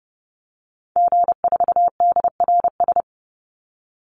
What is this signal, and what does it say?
Morse code, heard as a keyed tone, about 30 wpm, 700 Hz. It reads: G4DRS
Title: Call Signs - 30wpm